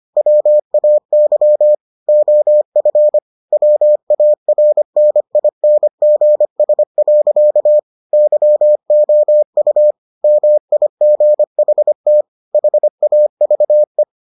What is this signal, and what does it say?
Morse code, heard as a keyed tone, about 25 wpm, 600 Hz. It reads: WAY OF WARNINGS. YOU MIGHT HAVE